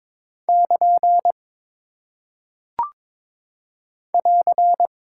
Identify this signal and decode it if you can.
Morse code, heard as a keyed tone, about 22 wpm, 700 Hz. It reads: <KN> <AR>
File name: practice / 22wpm / Single Punctuation-Prosign - 22wpm